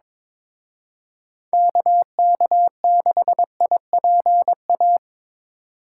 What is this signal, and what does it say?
Morse code, heard as a keyed tone, about 25 wpm, 700 Hz. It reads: KK6IPA
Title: Speed Racing - USA 2x3 Call Signs 22wpm